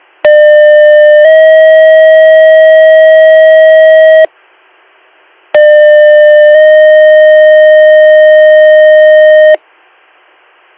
Two-tone page, heard as a one-second tone, then a three-second tone, about 600 then 640 Hz, about 600 then 620 Hz. It is heard from 0.2 to 9.5 s.